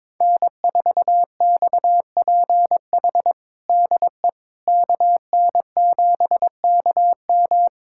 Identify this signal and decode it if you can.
Morse code, heard as a keyed tone, about 22 wpm, 700 Hz. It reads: N4XPH DE KN7KM
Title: USA Callsign DE Callsign - 5 Characters - 22wpm